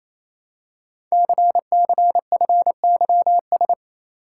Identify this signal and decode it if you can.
Morse code, heard as a keyed tone, about 29 wpm, 700 Hz. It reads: CCFYS
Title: Code Groups - 5 Letters - 28wpm